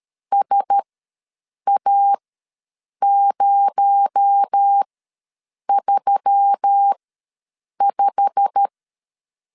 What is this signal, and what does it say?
Morse code, heard as a keyed tone, about 13 wpm, spaced as for 7 wpm, 800 Hz. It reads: SA035
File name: LEZIONI / 8 YJ45 / 8 YJ45 ws 12wpm 3146